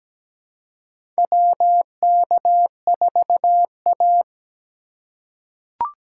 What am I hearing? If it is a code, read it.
Morse code, heard as a keyed tone, about 17 wpm, 700 Hz. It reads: WK4A
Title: USA Call Signs - 4 Characters - 3x Character Spacing - 17wpm